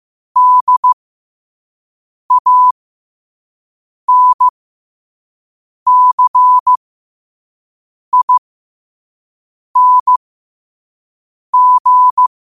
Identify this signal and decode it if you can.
Morse code, heard as a keyed tone, about 15 wpm, spaced as for 5 wpm, 1000 Hz. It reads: DANCING
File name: qotdfiles / 5wpm